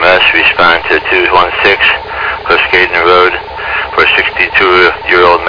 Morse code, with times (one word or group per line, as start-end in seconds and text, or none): none